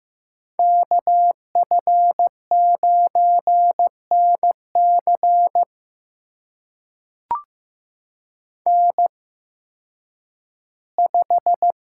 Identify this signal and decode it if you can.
Morse code, heard as a keyed tone, about 15 wpm, spaced as for 10 wpm, 700 Hz. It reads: KF9NC N5
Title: USA Call Signs - 5 Characters - 8x Character Spacing - 15wpm